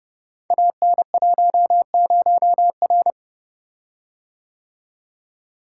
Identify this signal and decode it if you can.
Morse code, heard as a keyed tone, about 30 wpm, 700 Hz. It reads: AN10R
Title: Code Groups - 5 Letter:Numbers - 30wpm